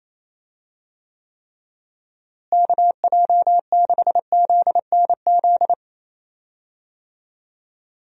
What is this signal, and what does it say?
Morse code, heard as a keyed tone, about 27 wpm, 700 Hz. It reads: KJ6ZNZ
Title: USA Call Signs - 6 Characters - Repeated 3x - 28wpm